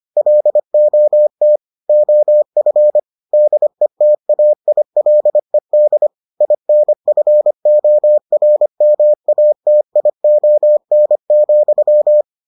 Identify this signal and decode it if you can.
Morse code, heard as a keyed tone, about 25 wpm, 600 Hz. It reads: LOT OF DETAILED INFORMATION,